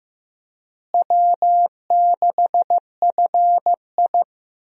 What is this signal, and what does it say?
Morse code, heard as a keyed tone, about 15 wpm, 700 Hz. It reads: W6FI
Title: USA Call Signs - 4 Characters - Repeated 3x - 15wpm